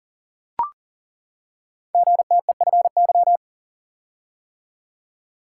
Morse code, heard as a keyed tone, about 42 wpm, 700 Hz. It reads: GTEFY
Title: Code Groups - 5 Letter:Numbers - 40wpm